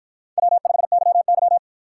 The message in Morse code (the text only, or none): W5YY